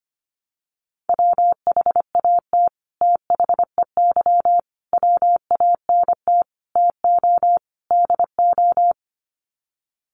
Morse code, heard as a keyed tone, about 25 wpm, 700 Hz. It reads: WHAT THEY WANT TO DO
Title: Sets of 5 Words Encore2 25wpm